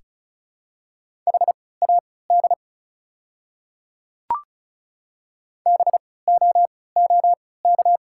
Morse code, heard as a keyed tone, about 34 wpm, spaced as for 20 wpm, 690 Hz. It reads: HAD BOOK
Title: 3x Character Spacing - Top 300 Words - 35wpm